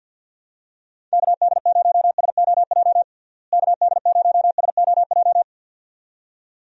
Morse code, heard as a keyed tone, about 47 wpm, 700 Hz. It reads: KD0SOJ KD0SOJ
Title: USA Call Signs - 6 Characters - Repeated 2x - 50wpm